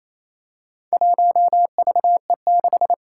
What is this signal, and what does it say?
Morse code, heard as a keyed tone, about 28 wpm, 700 Hz. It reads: 1VE6
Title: Four Challenging Letter-Number Combinations 28wpm